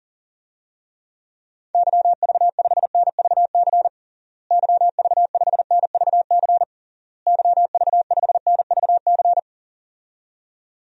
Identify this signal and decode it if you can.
Morse code, heard as a keyed tone, about 40 wpm, 700 Hz. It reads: YV5NVC YV5NVC YV5NVC
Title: DXCC-100 Call Signs - 6 Characters - Repeated 3x - 40wpm